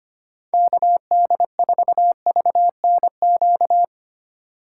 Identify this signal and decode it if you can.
Morse code, heard as a keyed tone, about 25 wpm, 700 Hz. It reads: KD4VNQ
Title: USA Call Signs - 2x Character Spacing - 25wpm